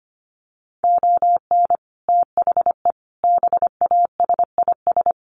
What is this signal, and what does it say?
Morse code, heard as a keyed tone, about 25 wpm, 700 Hz. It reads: ON THE BASIS
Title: Sets of 5 Words Encore 25wpm